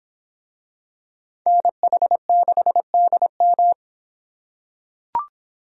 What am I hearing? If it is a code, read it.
Morse code, heard as a keyed tone, about 26 wpm, 700 Hz. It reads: NH6DM